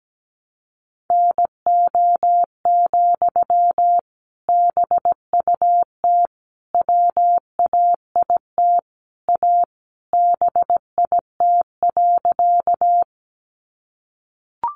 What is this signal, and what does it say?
Morse code, heard as a keyed tone, about 17 wpm, 700 Hz. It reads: NO, BUT WAIT A BIT.
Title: Sentences from Top 900 Words 17wpm